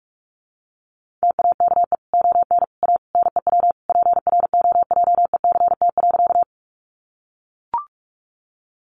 Morse code, heard as a keyed tone, about 44 wpm, 700 Hz. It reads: TAKE ON A NEW PROJECT.
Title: Sentences from Top 1400 Words 45wpm